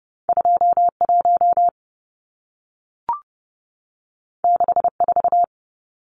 Morse code, heard as a keyed tone, about 30 wpm, 700 Hz. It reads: 21 64
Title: Numbers - 2-Digit 30wpm